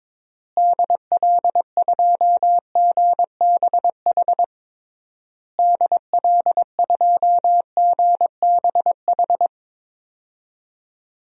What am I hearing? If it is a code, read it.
Morse code, heard as a keyed tone, about 21 wpm, 700 Hz. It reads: DL2GBH DL2GBH
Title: DXCC-100 Call Signs - 6 Characters - Repeated 3x - 22wpm